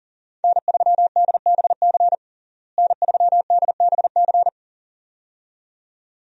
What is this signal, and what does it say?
Morse code, heard as a keyed tone, about 40 wpm, 700 Hz. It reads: N3DBC N3DBC
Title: USA Call Signs - 5 Characters - Repeated 2x - 40wpm